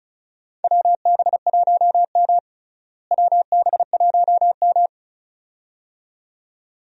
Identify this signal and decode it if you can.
Morse code, heard as a keyed tone, about 36 wpm, 700 Hz. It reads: WB1M WB1M
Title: USA Call Signs - 4 Characters - Repeated 2x - 35wpm